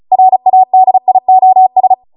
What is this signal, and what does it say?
Morse code, heard as a keyed tone, about 36 wpm, 750 Hz. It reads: RADIOS